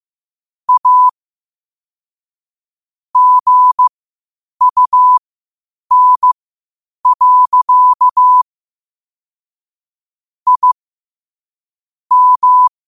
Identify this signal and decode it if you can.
Morse code, heard as a keyed tone, about 15 wpm, spaced as for 8 wpm, 1000 Hz. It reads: A GUN. I M
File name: qotdfiles / 8wpm